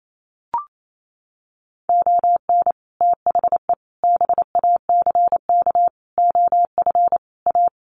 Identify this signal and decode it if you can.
Morse code, heard as a keyed tone, about 28 wpm, 700 Hz. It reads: ON THE BACK OF A
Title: Sets of 5 Words Encore2 28wpm